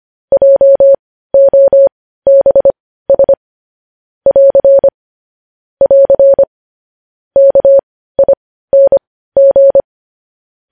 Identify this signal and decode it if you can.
Morse code, heard as a keyed tone, about 25 wpm, spaced as for 15 wpm, 550 Hz. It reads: JOBS <AR> <AR> KING
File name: podcast / News headlines-1x-25x15